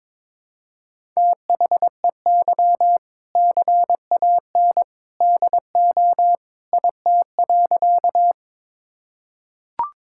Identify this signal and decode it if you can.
Morse code, heard as a keyed tone, about 22 wpm, 700 Hz. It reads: THEY CAN DO IT.
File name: practice / 22wpm / Sentences with 2-Word Phrases - 22wpm